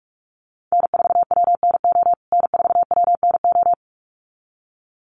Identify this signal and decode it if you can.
Morse code, heard as a keyed tone, about 44 wpm, 700 Hz. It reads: N4WNO N4WNO
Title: USA Call Signs Repeated 2x 45wpm